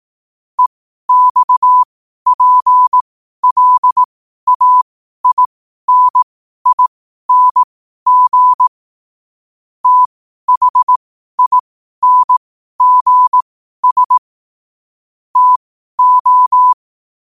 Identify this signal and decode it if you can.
Morse code, heard as a keyed tone, about 18 wpm, spaced as for 12 wpm, 1000 Hz. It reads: EXPLAINING THINGS TO
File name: qotdfiles / 12wpm